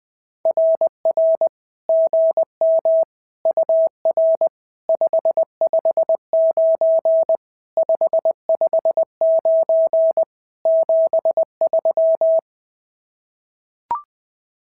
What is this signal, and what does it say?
Morse code, heard as a keyed tone, about 20 wpm, 650 Hz. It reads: RR GM UR 559 559 73
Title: SOTA QSO Practice 20wpm